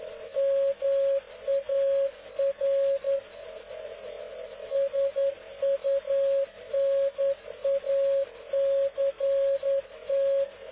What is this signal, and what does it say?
Morse code, heard as a keyed tone, about 10 wpm, 550 Hz. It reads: MAR SUNACT
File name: cw / ubacw dk0wcy slow